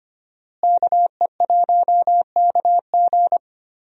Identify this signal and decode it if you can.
Morse code, heard as a keyed tone, about 25 wpm, 700 Hz. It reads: KE1KG
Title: USA Call Signs - 5 Characters - Repeated 2x - 25wpm